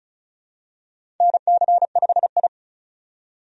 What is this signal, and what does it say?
Morse code, heard as a keyed tone, about 36 wpm, 700 Hz. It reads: NC5I
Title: USA Call Signs - 4 Characters - 35wpm